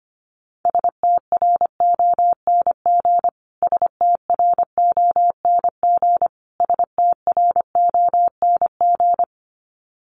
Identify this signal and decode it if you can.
Morse code, heard as a keyed tone, about 25 wpm, 700 Hz. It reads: STRONG STRONG STRONG
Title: Top 500 Words Repeated 25wpm